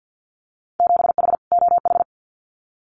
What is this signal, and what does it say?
Morse code, heard as a keyed tone, about 50 wpm, 700 Hz. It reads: 7H OH
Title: Winter Field Day Exchanges Encore 50wpm